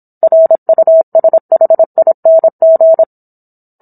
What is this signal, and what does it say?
Morse code, heard as a keyed tone, about 27 wpm, 650 Hz. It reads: RUSHING